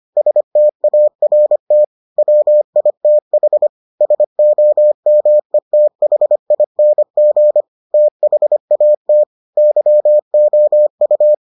Morse code, heard as a keyed tone, about 25 wpm, 600 Hz. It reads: START WITH SOMETHING THAT YOU